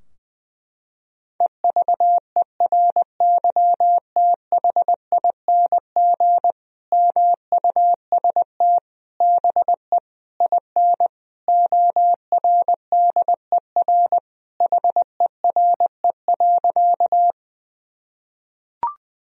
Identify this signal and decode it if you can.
Morse code, heard as a keyed tone, about 20 wpm, 700 Hz. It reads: EVERYTHING MUST BE IN ORDER HERE.
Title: Sentences from Top 1000 Words - Review Part 2 - 20wpm